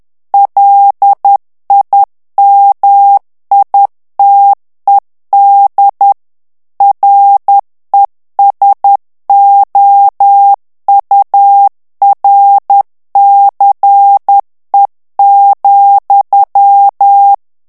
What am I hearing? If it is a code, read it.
Morse code, heard as a keyed tone, about 10 wpm, 800 Hz. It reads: LIMITED RESOURCE,
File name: ANGLAIS texte clair 3 vitesse 10 mots minute